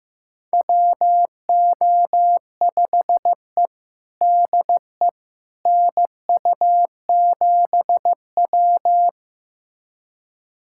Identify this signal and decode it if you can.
Morse code, heard as a keyed tone, about 15 wpm, 700 Hz. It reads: WO5E DE NU7W
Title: USA Callsign DE Callsign - 4 Characters - 15wpm